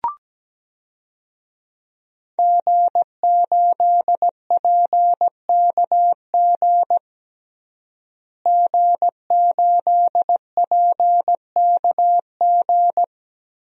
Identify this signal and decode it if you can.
Morse code, heard as a keyed tone, about 17 wpm, 700 Hz. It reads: G8PKG G8PKG
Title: DXCC-100 Call Signs - 5 Characters - Repeated 3x - 17wpm